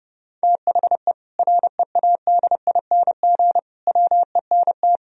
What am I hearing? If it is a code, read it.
Morse code, heard as a keyed tone, about 30 wpm, 700 Hz. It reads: THE READING WENT
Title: Sentences from Top 1000 Words - Review Part 3 - 30wpm